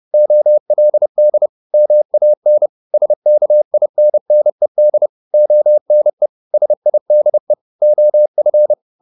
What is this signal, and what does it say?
Morse code, heard as a keyed tone, about 30 wpm, 600 Hz. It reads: OLD MAN SKINNED ONE SIDE OF